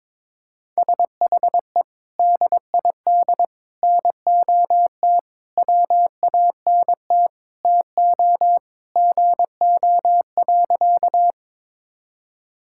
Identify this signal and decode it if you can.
Morse code, heard as a keyed tone, about 21 wpm, 700 Hz. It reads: SHE DID NOT WANT TO GO.
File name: practice / 22wpm / Sentences with 4-Word Phrases - 22wpm